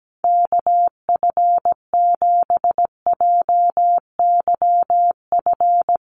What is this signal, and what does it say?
Morse code, heard as a keyed tone, about 17 wpm, 700 Hz. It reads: KF7JYF
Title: USA Call Signs 17wpm